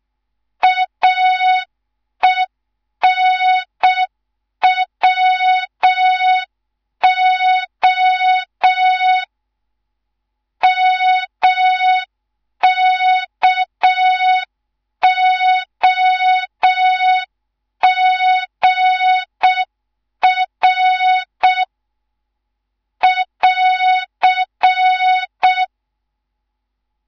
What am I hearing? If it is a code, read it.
Morse code, heard as a keyed tone, about 6 wpm, 3700 Hz. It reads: AENWO MKOGR <AR>
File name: Morze-HA2MN / hang / 11 GYAKORLAT